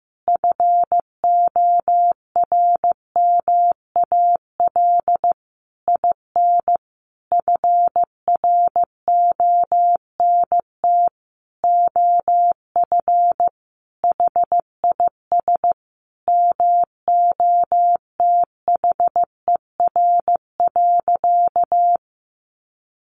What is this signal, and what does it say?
Morse code, heard as a keyed tone, about 15 wpm, 700 Hz. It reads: FORMAL IN FRONT OF HIS MOTHER.
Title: Sentences from Top 1300 Words 15wpm